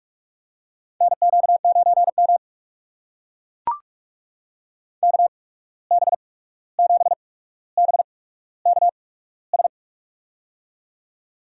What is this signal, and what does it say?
Morse code, heard as a keyed tone, about 41 wpm, spaced as for 16 wpm, 700 Hz. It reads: NQ9M KB7BKS
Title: USA Call Signs - 8x Character Spacing - 45wpm